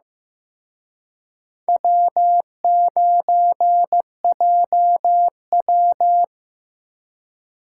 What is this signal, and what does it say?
Morse code, heard as a keyed tone, about 18 wpm, 700 Hz. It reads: W9JW
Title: Speed Racing - USA 1x2 Call Signs 15wpm